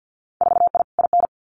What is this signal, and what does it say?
Morse code, heard as a keyed tone, about 50 wpm, 700 Hz. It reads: 4I IN
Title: Winter Field Day Exchanges Encore 50wpm